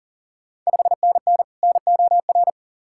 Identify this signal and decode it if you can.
Morse code, heard as a keyed tone, about 38 wpm, 700 Hz. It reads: 5NN NOR
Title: Ontario QSO Party Exchange 40wpm